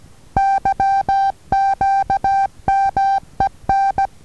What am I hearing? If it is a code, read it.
Morse code, heard as a keyed tone, about 16 wpm, 780 Hz. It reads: YQMEN
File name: Taviro HA6VA B